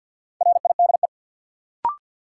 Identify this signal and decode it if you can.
Morse code, heard as a keyed tone, about 52 wpm, 700 Hz. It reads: RIDE
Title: Sentences from Top 1000 Words - Review Part 6 - New Word Drill - 50wpm